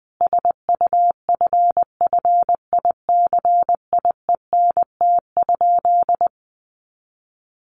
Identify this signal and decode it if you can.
Morse code, heard as a keyed tone, about 20 wpm, 700 Hz. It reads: SUFFICIENT?
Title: Sentences from Top 1400 Words 20wpm